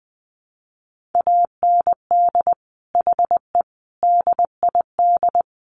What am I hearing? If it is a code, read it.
Morse code, heard as a keyed tone, about 20 wpm, 700 Hz. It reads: AND HE DID
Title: Sentences from Top 1500 Words 20wpm